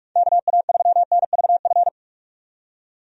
Morse code, heard as a keyed tone, about 41 wpm, 700 Hz. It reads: KA3NVF
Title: USA Call Signs - 6 Characters - 45wpm